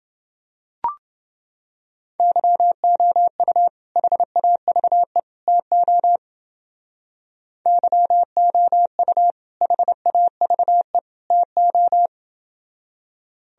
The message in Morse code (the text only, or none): YOU HAVE TO YOU HAVE TO